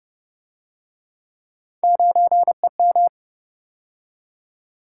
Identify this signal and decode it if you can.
Morse code, heard as a keyed tone, about 33 wpm, 700 Hz. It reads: 9EM